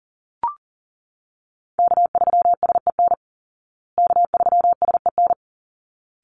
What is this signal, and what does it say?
Morse code, heard as a keyed tone, about 37 wpm, 700 Hz. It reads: K3SEN K3SEN
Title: Call Signs Repeated 3x - 40wpm